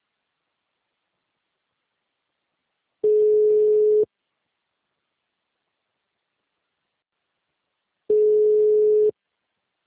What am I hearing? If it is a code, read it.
Ringback tone (Europe, 425 Hz).
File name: out- -2004- - - .82991